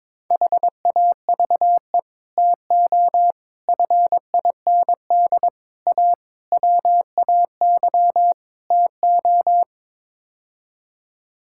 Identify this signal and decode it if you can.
Morse code, heard as a keyed tone, about 22 wpm, 700 Hz. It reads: HAVE TO FIND A WAY TO
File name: practice / 22wpm / Sets of 6 Words - 22wpm